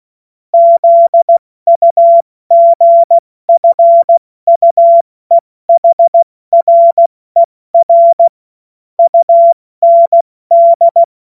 Morse code, heard as a keyed tone, about 16 wpm, 680 Hz. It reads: ZUGFUEHRER UND
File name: Morsekurs / Witze / Witz 05 Dezember 2023 70 F80